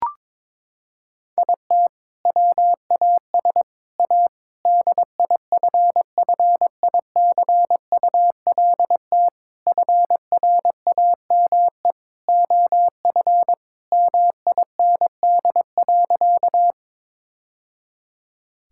Morse code, heard as a keyed tone, about 22 wpm, 700 Hz. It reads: IT WAS A DIFFICULT FRAME OF MIND.
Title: Sentences from Top 1000 Words - Review Part 6 - 22wpm